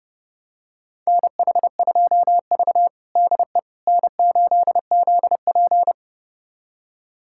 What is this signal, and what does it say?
Morse code, heard as a keyed tone, about 32 wpm, 700 Hz. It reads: NH2V DE N8ZP
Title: USA Callsign DE Callsign - 4 Characters - 30wpm